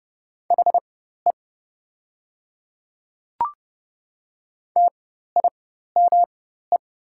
Morse code, heard as a keyed tone, about 30 wpm, spaced as for 14 wpm, 700 Hz. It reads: HE TIME
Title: 4x Character Spacing - Top 100 Words - 30wpm